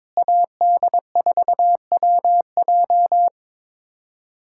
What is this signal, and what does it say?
Morse code, heard as a keyed tone, about 22 wpm, 700 Hz. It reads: AD4WJ
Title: USA Callsign DE Callsign 22wpm